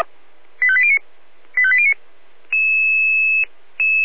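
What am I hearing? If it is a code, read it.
Five-tone selective call (ZVEI), heard as quick runs of five tones, tones of about 70 ms. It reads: 85809 85809
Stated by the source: Five-tone ZVEI selective call 85830 received by SDR.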